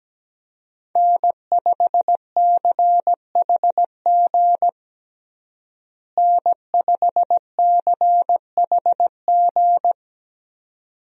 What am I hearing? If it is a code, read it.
Morse code, heard as a keyed tone, about 17 wpm, 700 Hz. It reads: N5CHG N5CHG
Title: USA Call Signs - 5 Characters - Repeated 2x - 17wpm